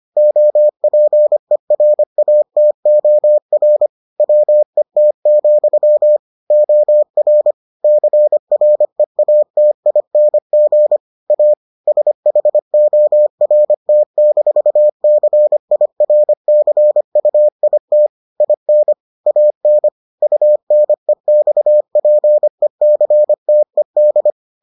Morse code, heard as a keyed tone, about 25 wpm, 600 Hz. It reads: OPERATOR WET, OR CREATING A SHORT-CIRCUIT IN AN UNEXPECTED